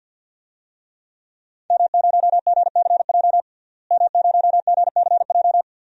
Morse code, heard as a keyed tone, about 47 wpm, 700 Hz. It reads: M0GCJ M0GCJ
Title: DXCC-100 Call Signs - 5 Characters - Repeated 2x - 50wpm